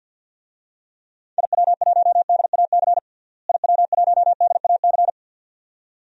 Morse code, heard as a keyed tone, about 46 wpm, 700 Hz. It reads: IW1DAC IW1DAC
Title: DXCC-100 Call Signs - 6 Characters - Repeated 2x - 50wpm